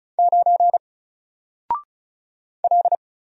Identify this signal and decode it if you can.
Morse code, heard as a keyed tone, about 36 wpm, 700 Hz. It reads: 9 L
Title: Single Character (Letter:Number:Punctuation:Prosign) Alt 35wpm